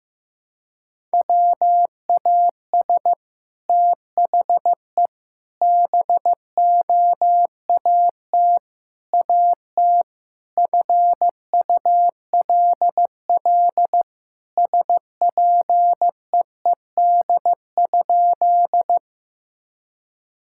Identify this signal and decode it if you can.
Morse code, heard as a keyed tone, about 15 wpm, 700 Hz. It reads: WAS THE BOAT AT FULL SPEED?
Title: Sentences from Top 1000 Words - Review Part 5 - 15wpm